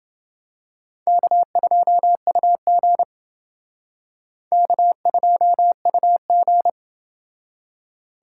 Morse code, heard as a keyed tone, about 28 wpm, 700 Hz. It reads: K2UG K2UG